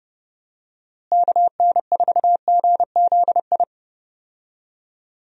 Morse code, heard as a keyed tone, about 32 wpm, 700 Hz. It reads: KN4GZI
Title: USA Call Signs - 6 Characters - Repeated 3x - 30wpm